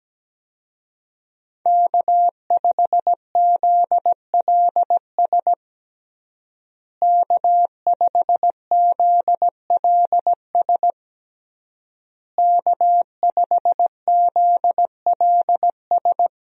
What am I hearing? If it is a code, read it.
Morse code, heard as a keyed tone, about 17 wpm, 700 Hz. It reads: K5ZLS K5ZLS K5ZLS
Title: USA Call Signs - 5 Characters - Repeated 3x - 17wpm